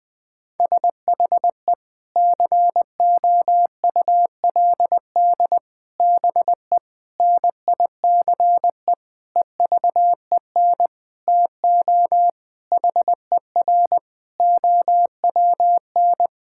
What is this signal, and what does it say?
Morse code, heard as a keyed tone, about 20 wpm, 700 Hz. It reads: SHE COULD BE NICE EVEN TO HER OWN
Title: Sentences from Top 1600 Words 20wpm